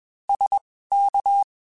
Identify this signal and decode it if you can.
Morse code, heard as a keyed tone, about 21 wpm, spaced as for 15 wpm, 800 Hz. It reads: SK
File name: (spaced) / QSOs Spaced-mp3s / WA7QOX 2 @20 sp